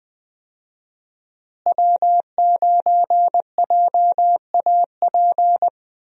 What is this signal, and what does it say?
Morse code, heard as a keyed tone, about 20 wpm, 700 Hz. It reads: W9JAP
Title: USA Call Signs - 5 Characters - Repeated 3x - 20wpm